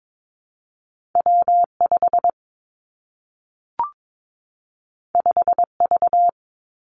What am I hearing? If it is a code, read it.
Morse code, heard as a keyed tone, about 22 wpm, 700 Hz. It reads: W 5 5 V
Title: Two Challenging Letter-Number Combinations 22wpm